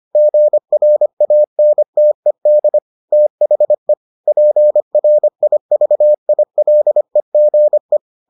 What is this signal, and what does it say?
Morse code, heard as a keyed tone, about 25 wpm, 600 Hz. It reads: GRANTED THE PRIVILEGE